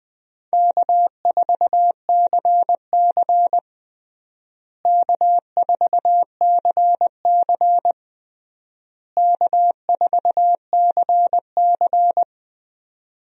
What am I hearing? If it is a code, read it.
Morse code, heard as a keyed tone, about 20 wpm, 700 Hz. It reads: K4CC K4CC K4CC
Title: USA Call Signs - 4 Characters - Repeated 3x - 20wpm